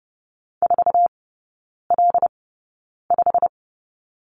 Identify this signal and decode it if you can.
Morse code, heard as a keyed tone, about 30 wpm, spaced as for 9 wpm, 700 Hz. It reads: 4L5
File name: course / 30:10wpm / 087 Lesson 15e - Number 5 - Three Characters 30wpm